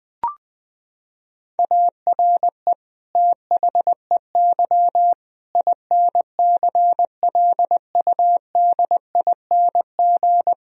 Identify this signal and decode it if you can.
Morse code, heard as a keyed tone, about 20 wpm, 700 Hz. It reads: ARE THEY INCLUDING
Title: Sentences from Top 1000 Words - Review Part 4 - 20wpm